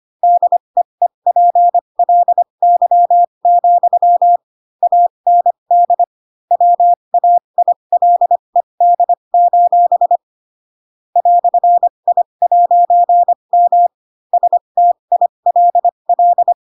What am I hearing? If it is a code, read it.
Morse code, heard as a keyed tone, about 25 wpm, spaced as for 22 wpm, 700 Hz. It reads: DEEPLY, AND WAILED: "I'M STILL
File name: Pinocchio / Pinocchio ch7 18 25 ik1hge